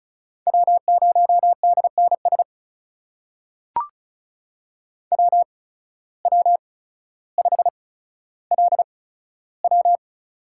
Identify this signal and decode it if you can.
Morse code, heard as a keyed tone, about 36 wpm, spaced as for 15 wpm, 700 Hz. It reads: W0DNS WW5LW
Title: USA Call Signs - 5 Characters - 8x Character Spacing - 35wpm